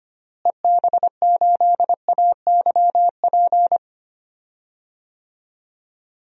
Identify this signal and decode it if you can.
Morse code, heard as a keyed tone, about 25 wpm, 700 Hz. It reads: EB8AYP
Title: DXCC-100 Call Signs - 6 Characters - Repeated 3x - 25wpm